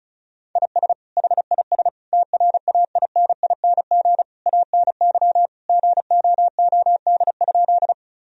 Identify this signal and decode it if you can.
Morse code, heard as a keyed tone, about 36 wpm, 700 Hz. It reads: IS HIS TRAINING ANY GOOD?
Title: Sentences from Top 1000 Words - Review Part 4 - 35wpm